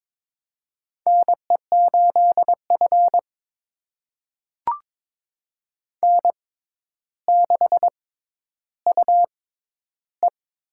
Morse code, heard as a keyed tone, about 22 wpm, spaced as for 11 wpm, 700 Hz. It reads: NE8F N6UE
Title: USA Call Signs - 4 Characters - 6x Character Spacing - 22wpm